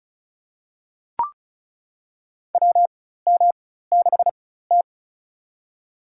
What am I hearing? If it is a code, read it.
Morse code, heard as a keyed tone, about 34 wpm, spaced as for 16 wpm, 700 Hz. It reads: WM6T
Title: USA Call Signs - 4 Characters - 4x Character Spacing - 35wpm